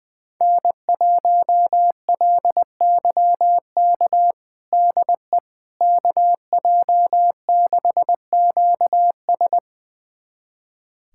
Morse code, heard as a keyed tone, about 20 wpm, 700 Hz. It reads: N1LYK DE KJ6QS
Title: USA Callsign DE Callsign - 5 Characters - 20wpm